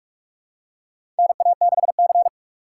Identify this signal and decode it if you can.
Morse code, heard as a keyed tone, about 46 wpm, 690 Hz. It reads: NA6C
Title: USA Call Signs - 4 Characters - 3x Character Spacing - 45wpm